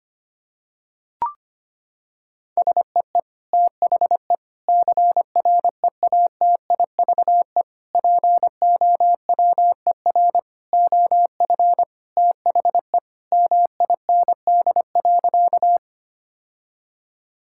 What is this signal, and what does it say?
Morse code, heard as a keyed tone, about 25 wpm, 700 Hz. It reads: SEE THE CREATIVE POWER OF THE MIND.
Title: Sentences from Top 1000 Words - Review Part 5 - 25wpm